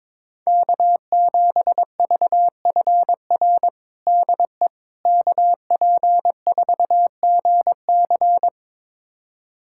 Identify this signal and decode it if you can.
Morse code, heard as a keyed tone, about 22 wpm, 700 Hz. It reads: K7VFR DE KP4GC
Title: USA Callsign DE Callsign - 5 Characters - 22wpm